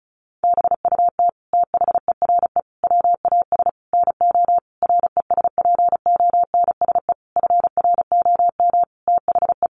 Morse code, heard as a keyed tone, about 34 wpm, 700 Hz. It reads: BUT THERE WAS NO RESPONSE FROM THE
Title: Sentences from Top 1400 Words 35wpm